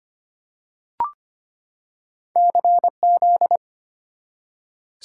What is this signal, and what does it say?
Morse code, heard as a keyed tone, about 25 wpm, 690 Hz. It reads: CZ
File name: Two Letter-Number Combinations - 25wpm